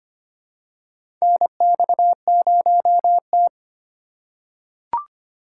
Morse code, heard as a keyed tone, about 25 wpm, 700 Hz. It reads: NX0T
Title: USA Call Signs - 4 Characters - 2x Character Spacing - 25wpm